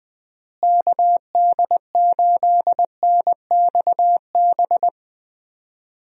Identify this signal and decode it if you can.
Morse code, heard as a keyed tone, about 20 wpm, 700 Hz. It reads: KD8NXB
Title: USA Call Signs - 6 Characters - Repeated 3x - 20wpm